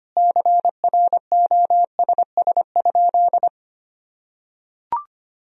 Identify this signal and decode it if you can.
Morse code, heard as a keyed tone, about 25 wpm, 700 Hz. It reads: CROSS?
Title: Sentences from Top 300 Words and Abbreviations 25wpm